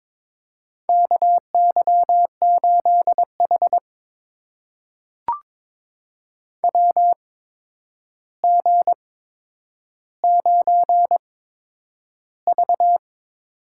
Morse code, heard as a keyed tone, about 22 wpm, spaced as for 9 wpm, 700 Hz. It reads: KY8H WG9V
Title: USA Call Signs - 4 Characters - 8x Character Spacing - 22wpm